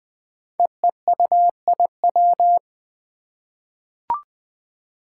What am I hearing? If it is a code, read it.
Morse code, heard as a keyed tone, about 20 wpm, 700 Hz. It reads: EEUIW
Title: Five Challenging Letter-Number Combinations 20wpm